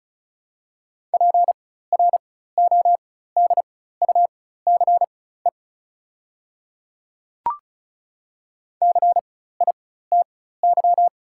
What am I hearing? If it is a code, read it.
Morse code, heard as a keyed tone, about 36 wpm, spaced as for 16 wpm, 700 Hz. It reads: PRODUCE CITY
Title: 4x Character Spacing - Top 500 Words - 35wpm